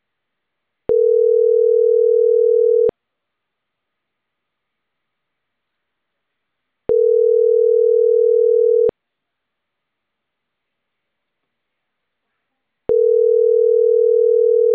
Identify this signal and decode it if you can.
Ringback tone (North America), 440 + 480 Hz.